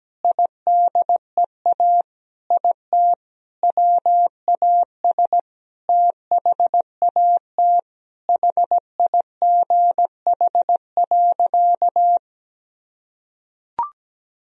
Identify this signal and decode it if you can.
Morse code, heard as a keyed tone, about 17 wpm, 700 Hz. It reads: IDEA IT WAS THAT HIGH.
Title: Sentences from Top 1000 Words - Review Part 1 - 17wpm